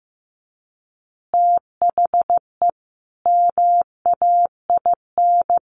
Morse code, heard as a keyed tone, about 15 wpm, 700 Hz. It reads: THE MAIN
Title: Additional Easy Phrases 15wpm